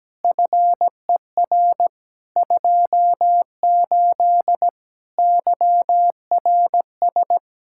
Morse code, heard as a keyed tone, about 17 wpm, 700 Hz. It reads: FER 28 YRS
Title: Practice QSO - Exchange 3 - More Info - 17wpm